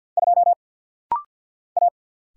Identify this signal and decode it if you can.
Morse code, heard as a keyed tone, about 50 wpm, 700 Hz. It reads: 2 A
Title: Single Letter-Number - Rapid-Fire 50wpm